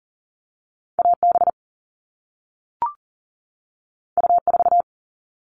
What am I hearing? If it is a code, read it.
Morse code, heard as a keyed tone, about 41 wpm, 700 Hz. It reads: AB U4
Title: All Cut Numbers - 2-Digit 40wpm